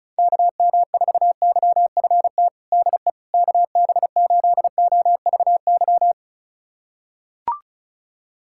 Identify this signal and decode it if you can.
Morse code, heard as a keyed tone, about 36 wpm, 700 Hz. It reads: KM4YFT DE KB8OVY
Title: USA Callsign DE Callsign - 6 Characters - 35wpm